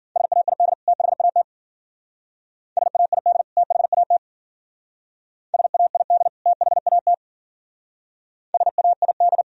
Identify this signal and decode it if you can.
Morse code, heard as a keyed tone, about 52 wpm, 700 Hz. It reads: SAID THAT SAID THAT SAID THAT SAID